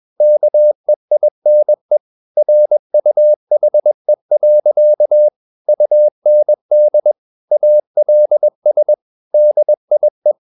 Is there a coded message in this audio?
Yes: Morse code from 0.2 to 10.3 s, heard as a keyed tone, about 21 wpm, 600 Hz. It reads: KEINE RUHE. UND ALS DIE